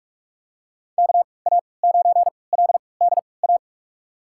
Morse code, heard as a keyed tone, about 46 wpm, spaced as for 25 wpm, 700 Hz. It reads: KA9LDA
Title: USA Call Signs - 6 Characters - 3x Character Spacing - 45wpm